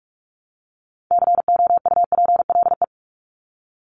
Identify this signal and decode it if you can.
Morse code, heard as a keyed tone, about 44 wpm, 700 Hz. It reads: COUPLE
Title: Top 1001-1250 Words 45wpm